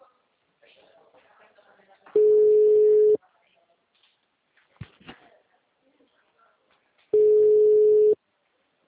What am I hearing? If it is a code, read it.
Ringback tone (Europe, 425 Hz), with speech.